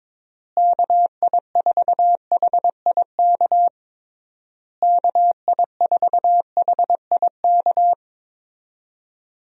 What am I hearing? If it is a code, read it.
Morse code, heard as a keyed tone, about 22 wpm, 700 Hz. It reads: KI4HIK KI4HIK
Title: USA Call Signs - 6 Characters - Repeated 3x - 22wpm